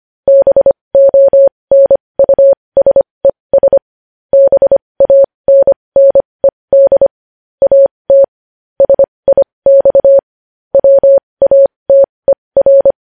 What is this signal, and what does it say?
Morse code, heard as a keyed tone, about 25 wpm, spaced as for 20 wpm, 550 Hz. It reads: BONUSES BANNED AT SIX WATER
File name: podcast / News headlines-1x-25x20